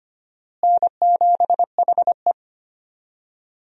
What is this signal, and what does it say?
Morse code, heard as a keyed tone, about 25 wpm, 700 Hz. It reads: N7HE
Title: USA Call Signs - 4 Characters - Repeated 2x - 25wpm